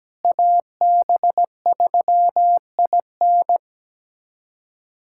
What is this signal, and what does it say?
Morse code, heard as a keyed tone, about 17 wpm, 700 Hz. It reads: AB3IN
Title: USA Callsign DE Callsign - 5 Characters - 17wpm